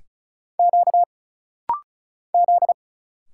Morse code, heard as a keyed tone, about 33 wpm, 700 Hz. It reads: Q Z
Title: Single Letters - Rapid-Fire 35wpm